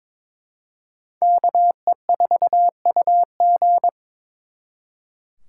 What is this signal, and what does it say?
Morse code, heard as a keyed tone, about 26 wpm, 700 Hz. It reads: KE4UG
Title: Speed Racing - USA 2x2 Call Signs 22wpm